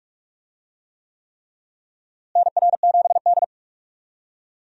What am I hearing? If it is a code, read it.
Morse code, heard as a keyed tone, about 46 wpm, 700 Hz. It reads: NR7D
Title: USA Call Signs - 4 Characters - Repeated 3x - 45wpm